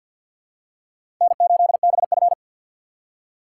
Morse code, heard as a keyed tone, about 47 wpm, 700 Hz. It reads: N8BF
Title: USA Call Signs - 4 Characters - 6x Character Spacing - 50wpm